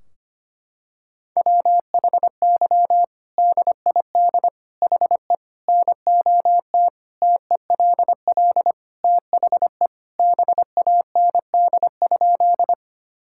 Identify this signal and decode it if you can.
Morse code, heard as a keyed tone, about 25 wpm, 700 Hz. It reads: WHY DID HE NOT TELL THE BAND?
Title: Sentences from Top 1600 Words 25wpm